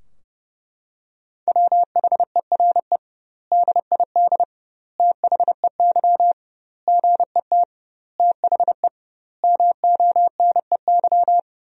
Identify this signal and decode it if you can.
Morse code, heard as a keyed tone, about 30 wpm, 700 Hz. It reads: WHERE DID THEY GET THE MONEY
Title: 2x Word-Spacing - Sentences from Top 400 Words - 30wpm